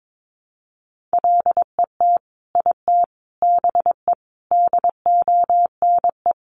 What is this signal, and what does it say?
Morse code, heard as a keyed tone, about 22 wpm, 700 Hz. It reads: LET IT BE DONE
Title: Sentences from Top 1100 Words 22wpm